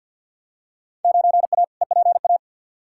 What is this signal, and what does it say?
Morse code, heard as a keyed tone, about 53 wpm, 700 Hz. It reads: 9A EPA
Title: Field Day Exchanges Encore - 50wpm